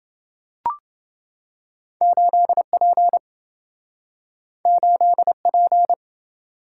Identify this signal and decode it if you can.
Morse code, heard as a keyed tone, about 27 wpm, 700 Hz. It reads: 8P 8P